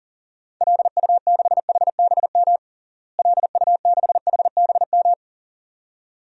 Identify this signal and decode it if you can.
Morse code, heard as a keyed tone, about 40 wpm, 700 Hz. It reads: LU6HBM LU6HBM
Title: DXCC-100 Call Signs - 6 Characters - Repeated 2x - 40wpm